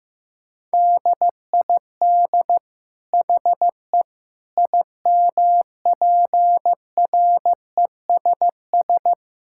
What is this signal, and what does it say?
Morse code, heard as a keyed tone, about 15 wpm, 700 Hz. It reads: DID HE IMPRESS
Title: Sentences from Top 1600 Words 15wpm